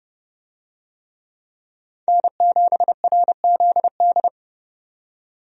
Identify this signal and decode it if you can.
Morse code, heard as a keyed tone, about 30 wpm, 700 Hz. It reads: N7RZD
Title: USA Call Signs - 5 Characters - 30wpm